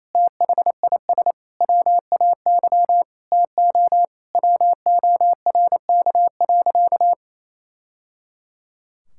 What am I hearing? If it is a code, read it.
Morse code, heard as a keyed tone, about 28 wpm, 700 Hz. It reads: THIS WAY TO WORK.
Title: Sentences from Top 1000 Words - Review Part 5 - 28wpm